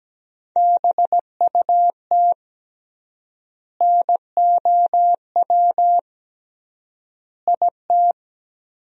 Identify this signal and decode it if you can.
Morse code, heard as a keyed tone, about 17 wpm, 700 Hz. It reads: BUT NOW IT
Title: 3x Word-Spacing - Sentences from Top 300 Words - 17wpm